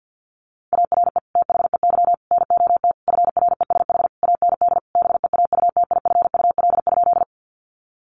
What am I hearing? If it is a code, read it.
Morse code, heard as a keyed tone, about 50 wpm, 700 Hz. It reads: ARE THEY NOT FRESH AND BEAUTIFUL?
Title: Sentences from Top 1200 Words 50wpm